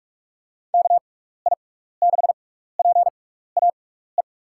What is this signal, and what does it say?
Morse code, heard as a keyed tone, about 46 wpm, spaced as for 16 wpm, 700 Hz. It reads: KI6PAE
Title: USA Call Signs - 6 Characters - 6x Character Spacing - 45wpm